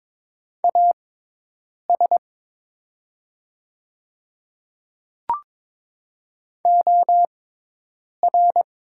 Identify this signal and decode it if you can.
Morse code, heard as a keyed tone, about 22 wpm, spaced as for 8 wpm, 700 Hz. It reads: AS OR
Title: 6x Character Spacing - Top 100 Words - 22wpm